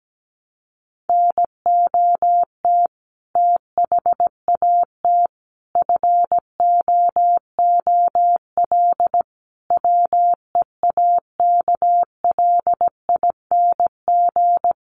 Morse code, heard as a keyed tone, about 17 wpm, 700 Hz. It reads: NOT THAT FOOL WEAKLING
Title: A Sound Of Thunder 17wpm